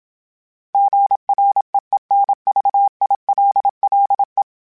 Morse code, heard as a keyed tone, about 27 wpm, 800 Hz. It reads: GREENVILLE
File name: fromSMs / FromSMs mp3s / Cities@25